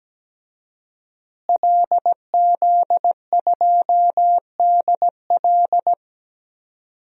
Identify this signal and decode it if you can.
Morse code, heard as a keyed tone, about 17 wpm, 700 Hz. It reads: LZ2DL
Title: DXCC-100 Call Signs - 5 Characters - Repeated 3x - 17wpm